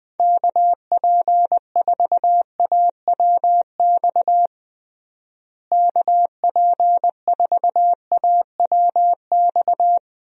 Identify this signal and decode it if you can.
Morse code, heard as a keyed tone, about 21 wpm, 700 Hz. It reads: KP4AWX KP4AWX
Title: DXCC-100 Call Signs - 6 Characters - Repeated 3x - 20wpm